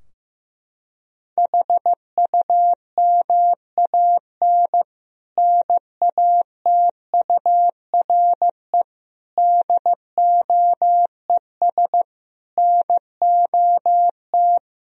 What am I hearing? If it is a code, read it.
Morse code, heard as a keyed tone, about 14 wpm, 700 Hz. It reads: HUMAN NATURE DOES NOT
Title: Sentences from Top 1000 Words - Review Part 3 - 15wpm